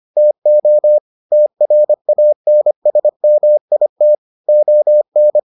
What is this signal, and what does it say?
Morse code, heard as a keyed tone, about 25 wpm, 600 Hz. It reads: TO TRANSMIT ON